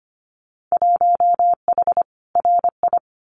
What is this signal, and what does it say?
Morse code, heard as a keyed tone, about 25 wpm, 700 Hz. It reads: 1H RI
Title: Winter Field Day Exchanges 25wpm